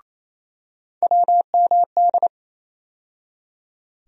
Morse code, heard as a keyed tone, about 27 wpm, 700 Hz. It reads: WMD
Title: USA Call Sign Postfixes - 3 Letters 28wpm